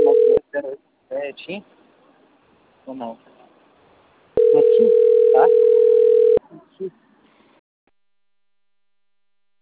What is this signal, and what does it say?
Ringback tone (North America), 440 + 480 Hz, with speech.
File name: rg-1600-unknown- - - .34968